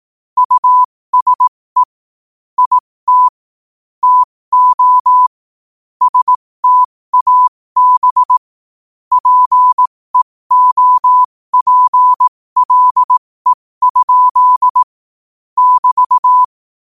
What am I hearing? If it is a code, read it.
Morse code, heard as a keyed tone, about 18 wpm, spaced as for 15 wpm, 1000 Hz. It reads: USE IT TO STAB PEOPLE? <BT>